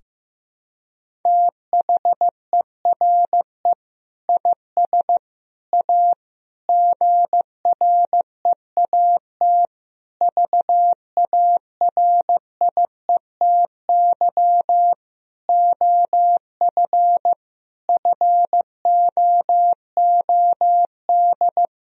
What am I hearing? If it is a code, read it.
Morse code, heard as a keyed tone, about 15 wpm, 700 Hz. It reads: THERE IS A GREAT VARIETY OF FOOD
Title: Sentences from Top 1000 Words - Review Part 4 - 15wpm